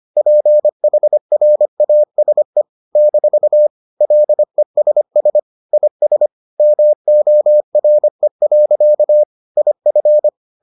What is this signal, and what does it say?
Morse code, heard as a keyed tone, about 25 wpm, 600 Hz. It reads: PHRASE - LESS IS MORE. IF